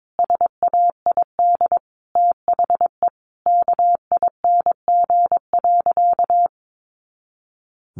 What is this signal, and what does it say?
Morse code, heard as a keyed tone, about 22 wpm, 700 Hz. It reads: SAID THE KING.
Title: Sentences from Top 400 Words 22wpm